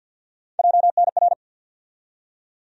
Morse code, heard as a keyed tone, about 51 wpm, 700 Hz. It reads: JNR
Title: USA Call Sign Postfixes - 3 Letters 50wpm